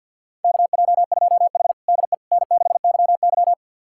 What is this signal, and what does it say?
Morse code, heard as a keyed tone, about 50 wpm, 700 Hz. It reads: KJ2H DE N6YY
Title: USA Callsign DE Callsign - 4 Characters - 50wpm